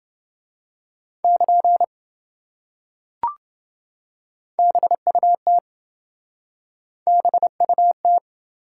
Morse code, heard as a keyed tone, about 26 wpm, 700 Hz. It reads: <KN> BUT BUT